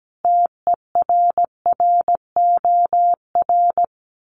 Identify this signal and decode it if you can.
Morse code, heard as a keyed tone, about 17 wpm, 700 Hz. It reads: TERROR
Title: Sets of 5 Words Encore2 17wpm